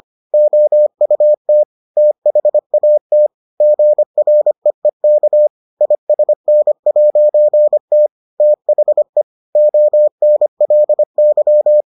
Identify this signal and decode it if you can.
Morse code, heard as a keyed tone, about 25 wpm, 600 Hz. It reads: OUT THAT GREEK ISN'T THE ONLY